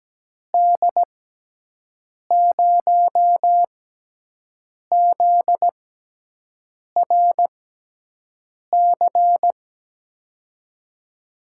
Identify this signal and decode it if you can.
Morse code, heard as a keyed tone, about 17 wpm, spaced as for 6 wpm, 700 Hz. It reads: D0ZRC
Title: USA Call Signs - 6 Characters - 6x Character Spacing - 17wpm